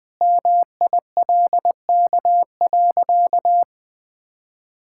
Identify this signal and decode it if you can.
Morse code, heard as a keyed tone, about 21 wpm, 700 Hz. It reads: MILK.
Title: Sentences with 2-Word Phrases - Spoken Beginning - 20wpm